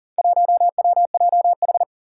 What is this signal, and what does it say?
Morse code, heard as a keyed tone, about 40 wpm, 700 Hz. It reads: 1WJH
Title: Four Challenging Letter-Number Combinations 40wpm